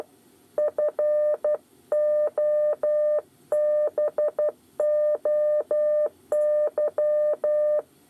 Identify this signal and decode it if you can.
Morse code, heard as a keyed tone, about 12 wpm, 600 Hz. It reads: FOBOY
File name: 8 bis lezione cw 2